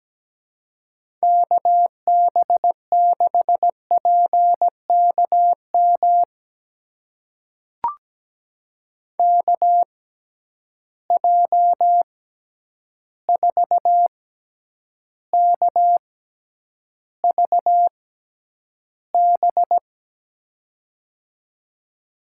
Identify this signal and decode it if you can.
Morse code, heard as a keyed tone, about 17 wpm, spaced as for 9 wpm, 700 Hz. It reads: KB6PKM KJ4KVB
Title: USA Call Signs - 6 Characters - 6x Character Spacing - 17wpm